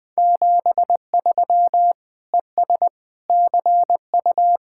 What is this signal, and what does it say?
Morse code, heard as a keyed tone, about 21 wpm, 700 Hz. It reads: 73 ES CU